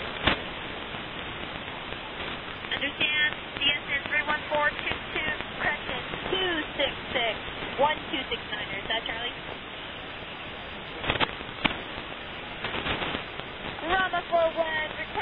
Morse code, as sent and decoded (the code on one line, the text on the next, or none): none